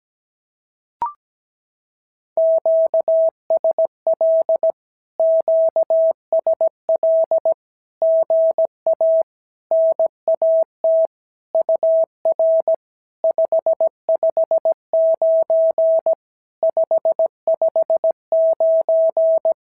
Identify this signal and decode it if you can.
Morse code, heard as a keyed tone, about 17 wpm, 650 Hz. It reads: QSL QSL GA NAT UR 559 559
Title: SOTA QSO Practice 17wpm